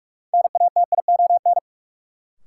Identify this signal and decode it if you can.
Morse code, heard as a keyed tone, about 43 wpm, 700 Hz. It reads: NATION
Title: Sentences from Top 900 Words - New Word Drill - 45wpm